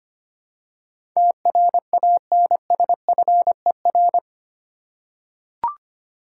Morse code, heard as a keyed tone, about 25 wpm, 700 Hz. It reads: TRANSFER
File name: 8 Letter Words - First 3 Letters Given - 25wpm